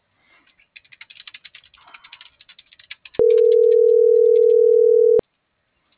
Ringback tone (North America), 440 + 480 Hz.